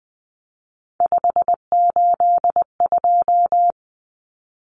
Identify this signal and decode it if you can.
Morse code, heard as a keyed tone, about 20 wpm, 700 Hz. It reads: 582
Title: Numbers - 3-Digit 20wpm